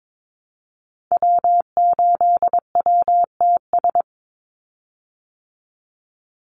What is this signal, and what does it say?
Morse code, heard as a keyed tone, about 22 wpm, 700 Hz. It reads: W8WTS
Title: Call Signs Repeated 2x - 22wpm